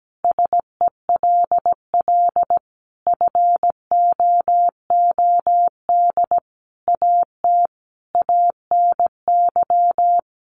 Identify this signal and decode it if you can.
Morse code, heard as a keyed tone, about 17 wpm, 700 Hz. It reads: SELL FOOD AT ANY
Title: Sentences from Top 1100 Words 17wpm